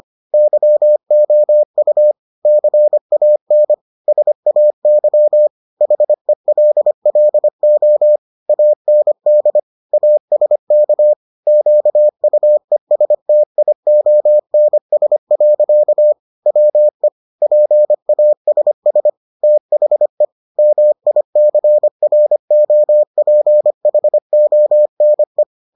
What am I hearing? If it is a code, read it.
Morse code, heard as a keyed tone, about 25 wpm, 600 Hz. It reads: YOU CAN SAY HELLO AND ASK QUESTIONS. WE PASS THE MICROPHONE